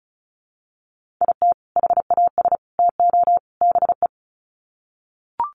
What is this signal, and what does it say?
Morse code, heard as a keyed tone, about 34 wpm, 700 Hz. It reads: IT HAS TO BE
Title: Sets of 4 Words 35wpm